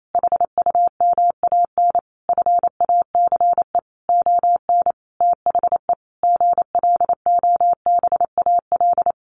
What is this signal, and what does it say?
Morse code, heard as a keyed tone, about 28 wpm, 700 Hz. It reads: HUMAN FACE ON THE GLOBAL
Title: Sentences from Top 1200 Words 28wpm